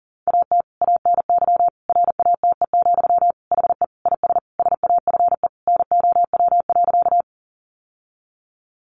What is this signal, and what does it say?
Morse code, heard as a keyed tone, about 40 wpm, 700 Hz. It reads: AT ANY RATE, HE IS SAFE NOW.
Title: Sentences from Top 900 Words 40wpm